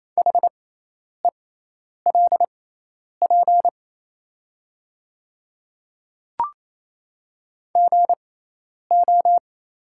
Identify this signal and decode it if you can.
Morse code, heard as a keyed tone, about 27 wpm, spaced as for 10 wpm, 700 Hz. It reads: HELP GO
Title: 6x Character Spacing - Top 300 Words - 28wpm